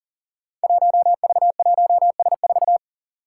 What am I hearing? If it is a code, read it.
Morse code, heard as a keyed tone, about 38 wpm, 700 Hz. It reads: 1V1S4
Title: Five Challenging Letter-Number Combinations 40wpm